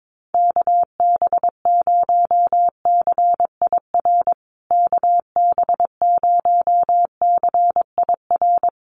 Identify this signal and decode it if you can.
Morse code, heard as a keyed tone, about 22 wpm, 700 Hz. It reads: KB0CIR KB0CIR
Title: USA Call Signs Repeated 3x 22wpm